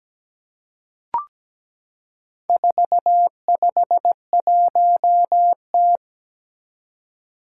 Morse code, heard as a keyed tone, about 17 wpm, 700 Hz. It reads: 451T
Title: Four Challenging Letter-Number Combinations 17wpm